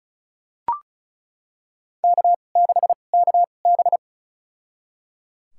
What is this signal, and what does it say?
Morse code, heard as a keyed tone, about 34 wpm, spaced as for 25 wpm, 700 Hz. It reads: K6KB
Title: USA Call Signs - 2x Character Spacing - 35wpm